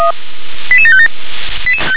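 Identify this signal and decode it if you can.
Five-tone selective call (ZVEI), heard as quick runs of five tones, tones of about 70 ms. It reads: 88757
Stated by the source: ZVEI five-tone selective call received on SDR.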